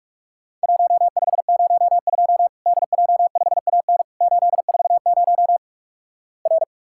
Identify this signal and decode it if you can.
Morse code, heard as a keyed tone, about 46 wpm, 700 Hz. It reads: 1502 DJ5AN 840 R
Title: Worked All Europe DX Contest - QTC Exchange - 45wpm